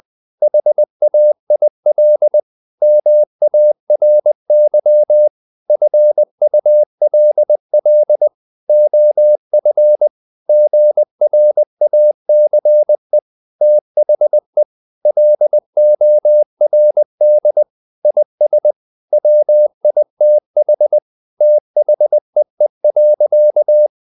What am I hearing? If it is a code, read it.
Morse code, heard as a keyed tone, about 20 wpm, 600 Hz. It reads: HAIL MARY FULL OF GRACE THE LORD IS WITH THEE.
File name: the old man and the sea / the old man and the sea part 1 20wpm